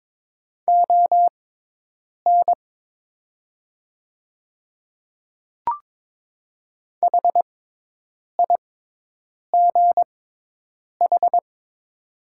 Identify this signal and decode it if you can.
Morse code, heard as a keyed tone, about 22 wpm, spaced as for 8 wpm, 700 Hz. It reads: ON HIGH
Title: 6x Character Spacing - Top 300 Words - 22wpm